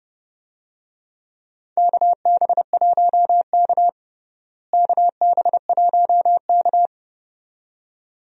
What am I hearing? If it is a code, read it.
Morse code, heard as a keyed tone, about 31 wpm, 700 Hz. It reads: KB1K KB1K
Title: USA Call Signs - 4 Characters - Repeated 2x - 30wpm